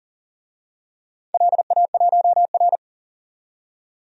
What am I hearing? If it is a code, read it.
Morse code, heard as a keyed tone, about 39 wpm, 700 Hz. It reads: LA1R
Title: DXCC-100 Call Signs - 4 Characters - 40wpm